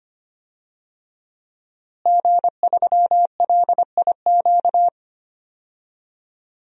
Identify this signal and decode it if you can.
Morse code, heard as a keyed tone, about 25 wpm, 700 Hz. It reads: G3LIQ
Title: DXCC-100 Call Signs - 5 Characters - Repeated 3x - 25wpm